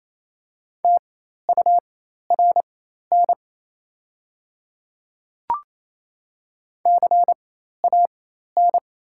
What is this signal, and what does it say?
Morse code, heard as a keyed tone, about 28 wpm, spaced as for 13 wpm, 690 Hz. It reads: TURN CAN
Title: 4x Character Spacing - Top 200 Words - 28wpm